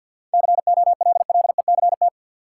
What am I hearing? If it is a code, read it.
Morse code, heard as a keyed tone, about 52 wpm, 700 Hz. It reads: COLLECT
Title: Sentences from Top 1600 Words - New Word Drill - 50wpm